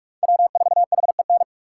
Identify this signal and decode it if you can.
Morse code, heard as a keyed tone, about 57 wpm, 700 Hz. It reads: W4HEN